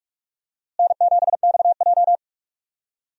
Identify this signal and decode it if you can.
Morse code, heard as a keyed tone, about 46 wpm, 700 Hz. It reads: N7XJ
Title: USA Call Signs - 4 Characters - Repeated 3x - 45wpm